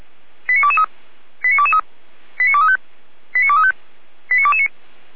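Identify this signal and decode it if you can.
Five-tone selective call (ZVEI), heard as quick runs of five tones, tones of about 70 ms. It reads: 89223 89223 89235 89235 89229
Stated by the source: ZVEI five-tone selective call received by SDR.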